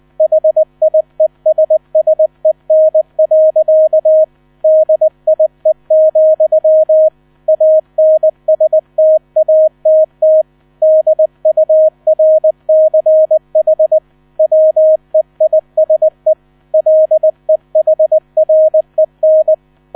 Morse code, heard as a keyed tone, about 20 wpm, 630 Hz. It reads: HIESSEN. DIE, ANSTATT DURCH WEISE LEHREN
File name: 000 Vorwort 20wpm